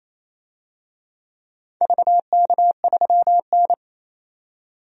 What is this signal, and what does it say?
Morse code, heard as a keyed tone, about 27 wpm, 700 Hz. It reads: VK3N
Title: DXCC-100 Call Signs - 4 Characters - 28wpm